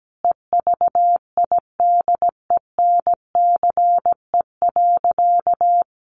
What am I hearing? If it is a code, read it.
Morse code, heard as a keyed tone, about 17 wpm, 700 Hz. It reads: EVIDENCE.
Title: Sentences from Top 1300 Words 17wpm